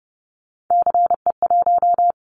Morse code, heard as a keyed tone, about 30 wpm, 700 Hz. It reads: CE1
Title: Callsign Prefixes for Top 100 DXCC - 30wpm